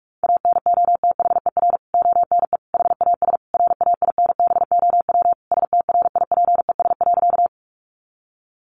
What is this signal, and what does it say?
Morse code, heard as a keyed tone, about 44 wpm, 700 Hz. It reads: ANOTHER ONE HAS RAINBOW STRIPES.
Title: Additional Easy Phrases 45wpm